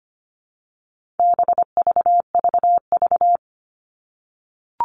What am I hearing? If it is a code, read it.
Morse code, heard as a keyed tone, about 25 wpm, 700 Hz. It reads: BVVV
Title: All Cut Numbers - 4-Digit 25wpm